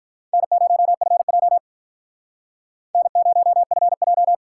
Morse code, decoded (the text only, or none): N0FJ N0FJ